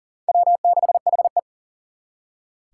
Morse code, heard as a keyed tone, about 52 wpm, 700 Hz. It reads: W6HE